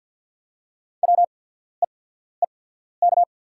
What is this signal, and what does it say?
Morse code, heard as a keyed tone, about 48 wpm, spaced as for 14 wpm, 690 Hz. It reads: WEEK
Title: 8x Character Spacing - Top 500 Words - 50wpm